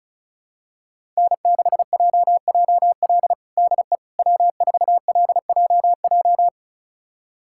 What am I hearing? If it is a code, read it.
Morse code, heard as a keyed tone, about 36 wpm, 700 Hz. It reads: N6JJL DE W4LJJ
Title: USA Callsign DE Callsign - 5 Characters - 35wpm